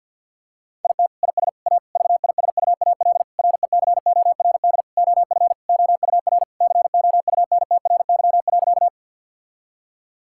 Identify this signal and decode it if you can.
Morse code, heard as a keyed tone, about 50 wpm, 700 Hz. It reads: IT IS A VISUAL RECORD OF OUR COUNTRY.
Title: Sentences from Top 1000 Words - Review Part 6 - 50wpm